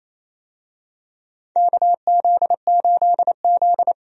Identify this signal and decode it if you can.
Morse code, heard as a keyed tone, about 28 wpm, 700 Hz. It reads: KZ8Z
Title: USA Call Signs - 4 Characters - 28wpm